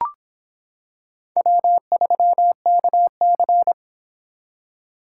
Morse code, heard as a keyed tone, about 26 wpm, 700 Hz. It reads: W3KC